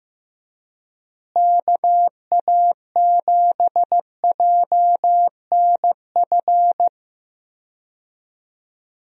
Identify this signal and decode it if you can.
Morse code, heard as a keyed tone, about 15 wpm, 700 Hz. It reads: KA7JNF
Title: USA Call Signs - 6 Characters - Repeated 2x - 15wpm